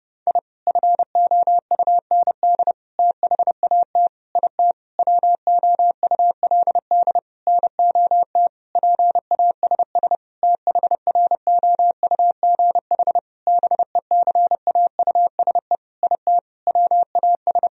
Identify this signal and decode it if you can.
Morse code, heard as a keyed tone, about 30 wpm, 700 Hz. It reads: I FOUND THAT IT WOULD NOT PASS THROUGH BECAUSE IT WAS
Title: Sentences from Top 1000 Words - Review Part 1 - 30wpm